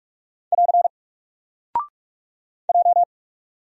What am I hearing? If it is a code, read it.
Morse code, heard as a keyed tone, about 46 wpm, 700 Hz. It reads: <AR> J
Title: Single Character (Letter:Number:Punctuation:Prosign) 45wpm